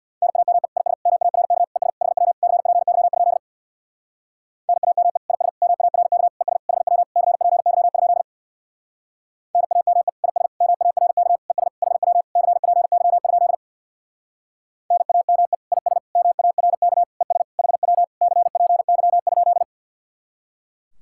Morse code, heard as a keyed tone, about 64 wpm, 700 Hz. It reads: NAME IS MARK ES HW CPY? NAME IS MARK ES HW CPY? NAME IS MARK ES HW CPY? NAME IS MARK ES HW CPY?